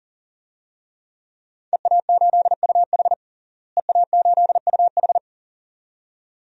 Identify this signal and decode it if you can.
Morse code, heard as a keyed tone, about 40 wpm, 700 Hz. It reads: EA8UH EA8UH
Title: DXCC-100 Call Signs - 5 Characters - Repeated 2x - 40wpm